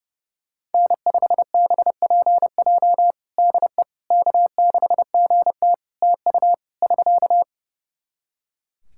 Morse code, heard as a keyed tone, about 30 wpm, 700 Hz. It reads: N5BPJ DE K6GT TU <SK>
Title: K1USN SST QSO Practice 30wpm